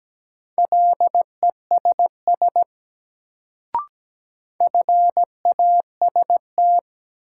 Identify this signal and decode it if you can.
Morse code, heard as a keyed tone, about 17 wpm, 700 Hz. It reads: LESS FAST
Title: Sentences from Top 400 Words - New Word Drill - 17wpm